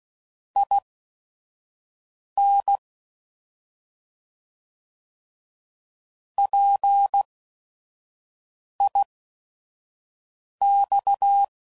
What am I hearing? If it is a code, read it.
Morse code, heard as a keyed tone, about 16 wpm, spaced as for 5 wpm, 800 Hz. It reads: IN PIX
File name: Prefixes / By First Letter / Ps